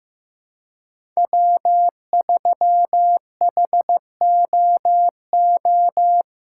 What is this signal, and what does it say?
Morse code, heard as a keyed tone, about 15 wpm, 700 Hz. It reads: W3HOO
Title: USA Callsign DE Callsign - 5 Characters - 15wpm